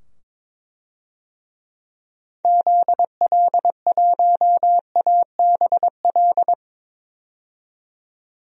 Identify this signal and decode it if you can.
Morse code, heard as a keyed tone, about 22 wpm, 700 Hz. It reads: ZL1ABL
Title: DXCC-100 Call Signs - 6 Characters - 22wpm